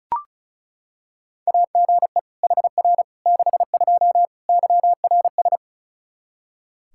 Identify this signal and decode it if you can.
Morse code, heard as a keyed tone, about 36 wpm, 700 Hz. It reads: AGE HR 62 YRS
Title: Practice QSO - Exchange 3 - More Info - 35wpm